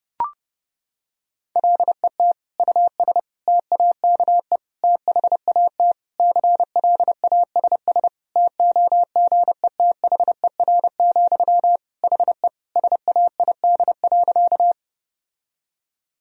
Morse code, heard as a keyed tone, about 32 wpm, 700 Hz. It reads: LET US TAKE THAT CLASS TOGETHER, HE SAID.
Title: Sentences from Top 1000 Words - Review Part 2 - 30wpm